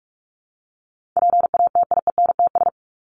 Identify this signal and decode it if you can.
Morse code, heard as a keyed tone, about 44 wpm, 700 Hz. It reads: PATIENTS
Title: ES:S Suffix - Top 100 Words with Base Words 45wpm